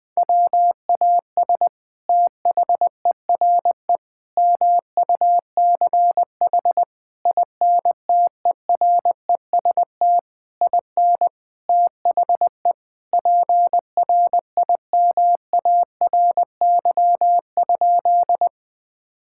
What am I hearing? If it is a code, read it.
Morse code, heard as a keyed tone, about 20 wpm, 700 Hz. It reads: WAS THERE MUCH INTEREST IN THE PRIMARY?
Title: Sentences from Top 1000 Words - Review Part 5 - 20wpm